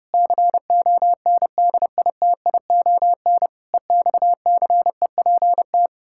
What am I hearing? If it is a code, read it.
Morse code, heard as a keyed tone, about 30 wpm, 700 Hz. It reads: CONDITION EXCEPT
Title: Pangram Challenge v2 - 30wpm